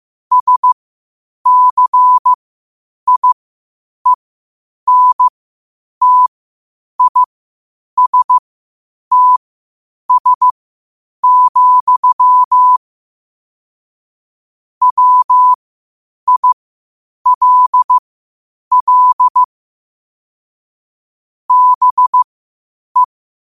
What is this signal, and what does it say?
Morse code, heard as a keyed tone, about 15 wpm, spaced as for 8 wpm, 1000 Hz. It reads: SCIENTISTS, WILL BE
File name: qotdfiles / 8wpm